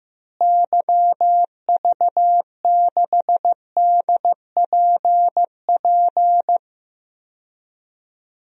Morse code, heard as a keyed tone, about 15 wpm, 700 Hz. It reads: YV6DPP
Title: DXCC-100 Call Signs - 6 Characters - Repeated 2x - 15wpm